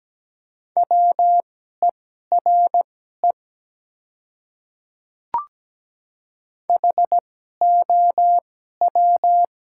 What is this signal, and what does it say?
Morse code, heard as a keyed tone, about 17 wpm, spaced as for 12 wpm, 700 Hz. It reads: WERE HOW
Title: 2x Character Spacing - Top 100 Words - 17wpm